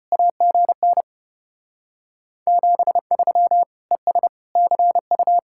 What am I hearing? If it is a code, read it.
Morse code, heard as a keyed tone, about 31 wpm, 700 Hz. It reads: AGN 73 ES CU